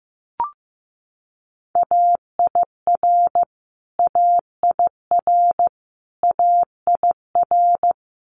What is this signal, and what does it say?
Morse code, heard as a keyed tone, about 15 wpm, 700 Hz. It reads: AIR AIR AIR
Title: Top 500 Words Repeated 15wpm